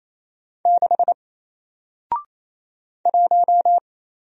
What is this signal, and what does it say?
Morse code, heard as a keyed tone, about 27 wpm, 700 Hz. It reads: 6 1
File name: Single-Letter-Number-28wpm